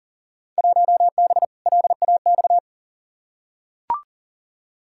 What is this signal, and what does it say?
Morse code, heard as a keyed tone, about 40 wpm, 700 Hz. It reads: 1B LAX
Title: Field Day Exchanges - 40wpm